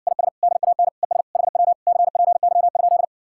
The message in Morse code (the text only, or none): IS DAN ES HW CPY?